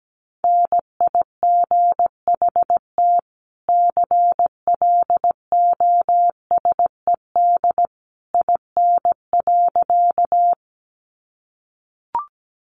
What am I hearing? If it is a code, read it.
Morse code, heard as a keyed tone, about 17 wpm, 700 Hz. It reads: NIGHT CLOSED IN.
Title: Sentences from Top 1400 Words 17wpm